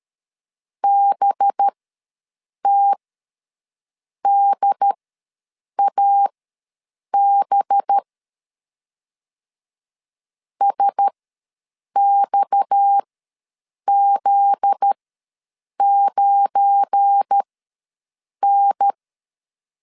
Morse code, heard as a keyed tone, about 13 wpm, spaced as for 7 wpm, 800 Hz. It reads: BTDAB SXZ9N
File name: LEZIONI / 7 GF9Z / 7 GF9Z ws 12wpm 3321